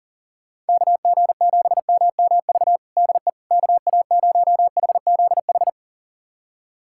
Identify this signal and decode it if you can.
Morse code, heard as a keyed tone, about 40 wpm, 700 Hz. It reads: KG7MMV DE KA0HZH
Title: USA Callsign DE Callsign 40wpm